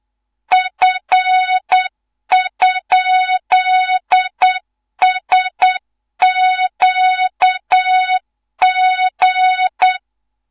Morse code, heard as a keyed tone, about 8 wpm, 3700 Hz, with noise. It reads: F?SQG
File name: Morze-HA2MN / hang / 15 GYAKORLAT